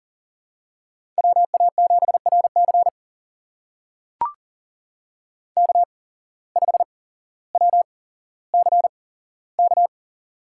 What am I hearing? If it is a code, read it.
Morse code, heard as a keyed tone, about 42 wpm, spaced as for 17 wpm, 700 Hz. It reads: WA7RC K5WCK
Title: USA Call Signs - 5 Characters - 8x Character Spacing - 40wpm